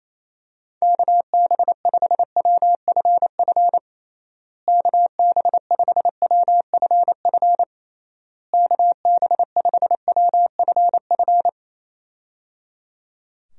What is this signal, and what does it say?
Morse code, heard as a keyed tone, about 28 wpm, 700 Hz. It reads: KB5WFF KB5WFF KB5WFF
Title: USA Call Signs - 6 Characters - Repeated 3x - 28wpm